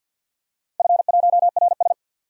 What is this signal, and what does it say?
Morse code, heard as a keyed tone, about 50 wpm, 700 Hz. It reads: F1RS
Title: DXCC-100 Call Signs - 4 Characters - 50wpm